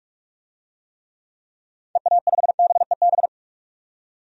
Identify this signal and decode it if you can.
Morse code, heard as a keyed tone, about 47 wpm, 690 Hz. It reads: EA5BEB
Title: DXCC-100 Call Signs - 6 Characters - 45wpm